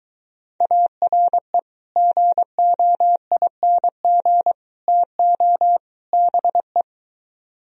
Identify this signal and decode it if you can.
Morse code, heard as a keyed tone, about 23 wpm, 700 Hz. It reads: ARE GOING TO BE